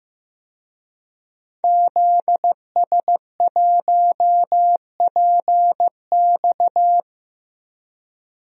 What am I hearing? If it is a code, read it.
Morse code, heard as a keyed tone, about 15 wpm, 700 Hz. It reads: ZS1PX
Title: DXCC-100 Call Signs - 5 Characters - Repeated 2x - 15wpm